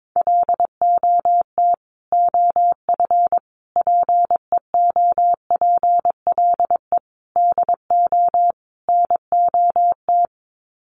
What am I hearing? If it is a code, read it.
Morse code, heard as a keyed tone, about 22 wpm, 700 Hz. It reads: LOT OF PEOPLE DO NOT
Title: Sets of 5 Words Encore 22wpm